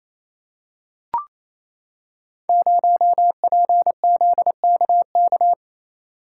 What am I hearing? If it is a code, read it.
Morse code, heard as a keyed tone, about 27 wpm, 690 Hz. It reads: 0PZKK
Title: Code Groups - 5 Letter:Numbers - 28wpm